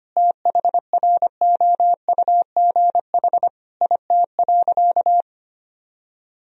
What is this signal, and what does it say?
Morse code, heard as a keyed tone, about 25 wpm, 700 Hz. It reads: THROUGH IT.
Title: Sentences from Top 1000 Words - Review Part 1 - 25wpm